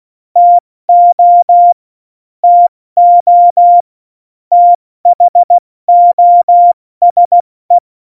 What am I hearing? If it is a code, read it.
Morse code, heard as a keyed tone, about 16 wpm, 700 Hz. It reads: TO TO THOSE
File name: Morsekurs / Witze / Witz 25 Juli 17, Tempo 70 F80